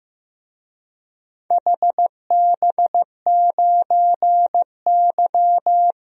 Morse code, heard as a keyed tone, about 15 wpm, 700 Hz. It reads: HB9Y
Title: DXCC-100 Call Signs - 4 Characters - Repeated 3x - 15wpm